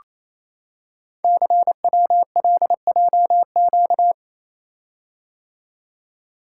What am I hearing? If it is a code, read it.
Morse code, heard as a keyed tone, about 29 wpm, 700 Hz. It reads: CWLJQ
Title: Code Groups - 5 Letters - 28wpm